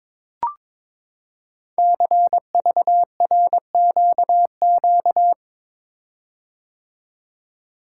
Morse code, heard as a keyed tone, about 22 wpm, 700 Hz. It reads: CVRQQ
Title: Code Groups - 5 Letters - 22wpm